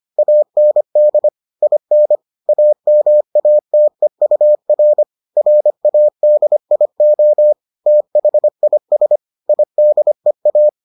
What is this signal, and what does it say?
Morse code, heard as a keyed tone, about 25 wpm, 600 Hz. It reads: AND IN AMATEUR RADIO THIS IDEA